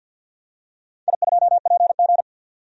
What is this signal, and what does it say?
Morse code, heard as a keyed tone, about 51 wpm, 700 Hz. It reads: I2PG
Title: DXCC-100 Call Signs - 4 Characters - 50wpm